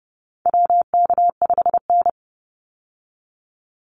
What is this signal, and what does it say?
Morse code, heard as a keyed tone, about 30 wpm, 700 Hz. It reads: WK5N
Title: Call Signs - 30wpm